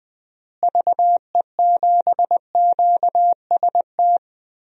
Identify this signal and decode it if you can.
Morse code, heard as a keyed tone, about 20 wpm, 700 Hz. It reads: VE7QST
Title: DXCC-100 Call Signs - 6 Characters - Repeated 3x - 20wpm